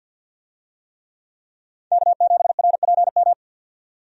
Morse code, heard as a keyed tone, about 52 wpm, 700 Hz. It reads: K7RPM
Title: USA Call Signs - 5 Characters - 50wpm